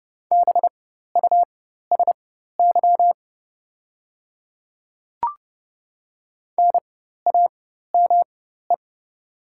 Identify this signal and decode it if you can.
Morse code, heard as a keyed tone, about 30 wpm, spaced as for 14 wpm, 700 Hz. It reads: BUSY NAME
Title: 4x Character Spacing - Top 500 Words - 30wpm